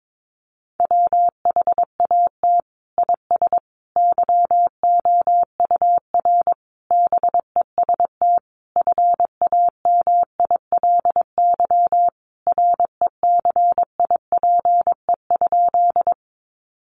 Morse code, heard as a keyed tone, about 22 wpm, 700 Hz. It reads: WHAT IS YOUR BEST FAMILY RECIPE?
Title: Sentences from Top 1200 Words 22wpm